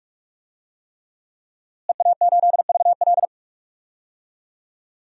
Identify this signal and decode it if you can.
Morse code, heard as a keyed tone, about 45 wpm, 700 Hz. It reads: EA8VL
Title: DXCC-100 Call Signs - 5 Characters - Repeated 3x - 45wpm